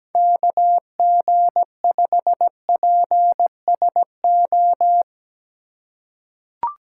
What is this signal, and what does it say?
Morse code, heard as a keyed tone, about 17 wpm, 700 Hz. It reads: KG5PSO
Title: USA Call Signs - 3x Character Spacing - 17wpm